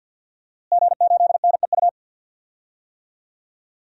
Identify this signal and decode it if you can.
Morse code, heard as a keyed tone, about 46 wpm, 700 Hz. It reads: G8NEU
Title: DXCC-100 Call Signs - 5 Characters - Repeated 2x - 50wpm